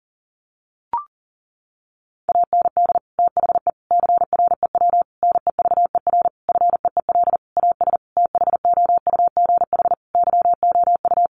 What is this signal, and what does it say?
Morse code, heard as a keyed tone, about 40 wpm, 700 Hz. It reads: AND THE CREW NEVER FEEL AS THOUGH YOU
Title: Sentences from Top 1500 Words 40wpm